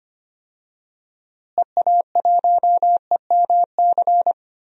Morse code, heard as a keyed tone, about 25 wpm, 700 Hz. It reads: EA1EMC
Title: DXCC-100 Call Signs - 6 Characters - Repeated 3x - 25wpm